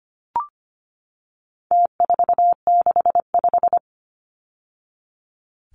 Morse code, heard as a keyed tone, about 25 wpm, 700 Hz. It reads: T465
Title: Numbers - Contest Style - 4-Digit 25wpm